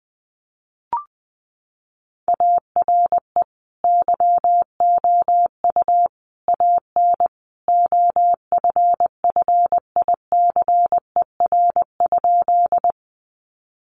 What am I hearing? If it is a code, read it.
Morse code, heard as a keyed tone, about 20 wpm, 700 Hz. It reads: ARE YOU AN OFFICER?
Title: Sentences from Top 1200 Words 20wpm